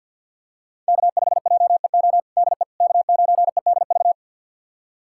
Morse code, heard as a keyed tone, about 47 wpm, 700 Hz. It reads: K5JEO DE K9EDV
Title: USA Callsign DE Callsign - 5 Characters - 50wpm